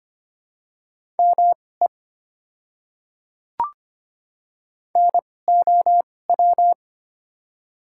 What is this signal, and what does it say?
Morse code, heard as a keyed tone, about 25 wpm, spaced as for 18 wpm, 700 Hz. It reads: ME NOW
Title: 2x Character Spacing - Top 200 Words - 25wpm